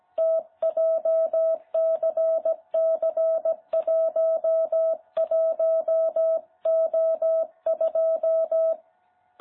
Morse code, heard as a keyed tone, about 17 wpm, 660 Hz. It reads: T JCC11O2
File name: jh8cbh / 2021cw